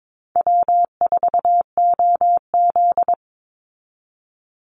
Morse code, heard as a keyed tone, about 22 wpm, 700 Hz. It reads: W4OZ
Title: USA Call Signs 22wpm